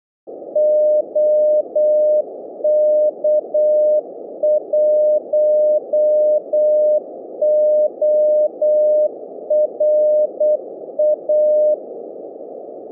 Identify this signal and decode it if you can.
Morse code, heard as a keyed tone, about 8 wpm, 610 Hz. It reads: OK1ORA